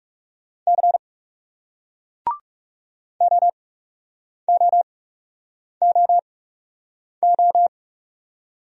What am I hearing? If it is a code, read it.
Morse code, heard as a keyed tone, about 40 wpm, spaced as for 8 wpm, 700 Hz. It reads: C OOOO